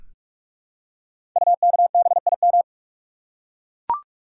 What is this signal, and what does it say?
Morse code, heard as a keyed tone, about 44 wpm, 700 Hz. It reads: UKBIM
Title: Code Groups - 5 Letters - 45wpm